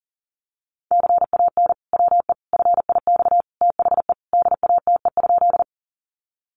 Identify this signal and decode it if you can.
Morse code, heard as a keyed tone, about 37 wpm, 700 Hz. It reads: CAN WE FIX THE DATE?
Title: Sentences from Top 1400 Words 40wpm